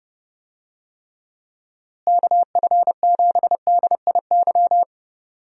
Morse code, heard as a keyed tone, about 31 wpm, 700 Hz. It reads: KF7DIY
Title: USA Call Signs - 6 Characters - Repeated 2x - 30wpm